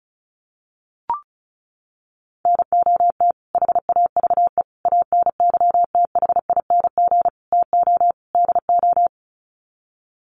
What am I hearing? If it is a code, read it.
Morse code, heard as a keyed tone, about 34 wpm, 700 Hz. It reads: NOT HAVE ANYTHING TO DO
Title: Sets of 5 Words Encore 35wpm